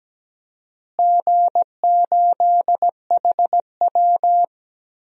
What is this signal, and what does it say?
Morse code, heard as a keyed tone, about 17 wpm, 700 Hz. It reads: G8HW
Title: DXCC-100 Call Signs - 4 Characters - Repeated 2x - 17wpm